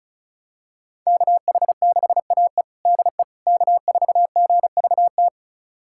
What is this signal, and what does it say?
Morse code, heard as a keyed tone, about 36 wpm, 700 Hz. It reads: KH6AE DE K4GVT
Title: USA Callsign DE Callsign - 5 Characters - 35wpm